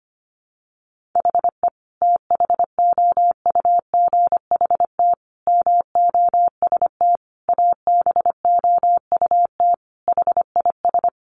Morse code, heard as a keyed tone, about 25 wpm, 700 Hz. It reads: HE THOUGHT MOST ABOUT HIS
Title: Sentences from Top 1500 Words 25wpm